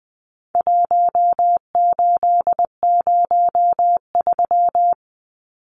Morse code, heard as a keyed tone, about 20 wpm, 700 Hz. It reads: 1803
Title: Numbers - 4-Digit 20wpm